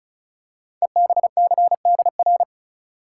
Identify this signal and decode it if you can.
Morse code, heard as a keyed tone, about 36 wpm, 690 Hz. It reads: EBCDR
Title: Code Groups - 5 Letters - 35wpm